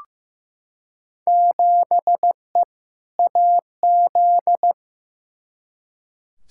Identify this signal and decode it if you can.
Morse code, heard as a keyed tone, about 15 wpm, 700 Hz. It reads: 7E AZ
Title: Field Day Exchanges Encore - 15wpm